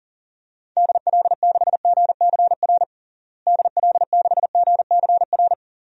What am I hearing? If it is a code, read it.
Morse code, heard as a keyed tone, about 40 wpm, 700 Hz. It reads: DL6GCR DL6GCR
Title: DXCC-100 Call Signs - 6 Characters - Repeated 3x - 40wpm